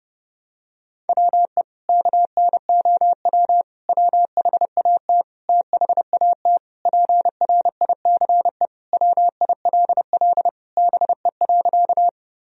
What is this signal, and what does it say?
Morse code, heard as a keyed tone, about 30 wpm, 700 Hz. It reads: WE KNOW WHAT THAT PRICE WILL BE.
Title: Sentences from Top 1000 Words - Review Part 4 - 30wpm